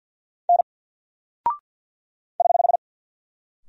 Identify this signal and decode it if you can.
Morse code, heard as a keyed tone, about 50 wpm, 700 Hz. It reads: N <HH>
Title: Single Character (Letter:Number:Punctuation:Prosign) 50wpm